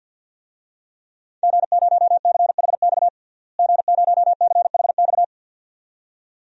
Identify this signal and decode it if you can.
Morse code, heard as a keyed tone, about 51 wpm, 700 Hz. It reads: G0CHX G0CHX
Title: DXCC-100 Call Signs - 5 Characters - Repeated 2x - 50wpm